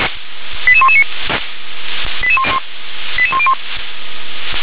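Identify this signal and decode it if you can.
Five-tone selective call (ZVEI), heard as quick runs of five tones, tones of about 70 ms. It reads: 88119 88101
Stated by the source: Five-tone ZVEI selective call received by SDR.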